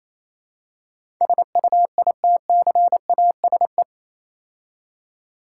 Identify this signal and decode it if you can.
Morse code, heard as a keyed tone, about 27 wpm, 700 Hz. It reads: SUITCASE
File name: Easy Words - Speed-Racing with Hints - 22wpm